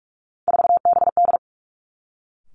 Morse code, heard as a keyed tone, about 46 wpm, 700 Hz. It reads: 4BD
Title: Three Challenging Letter-Number Combinations 45wpm